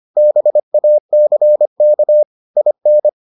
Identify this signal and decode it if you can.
Morse code, heard as a keyed tone, about 25 wpm, 600 Hz. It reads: BACK IN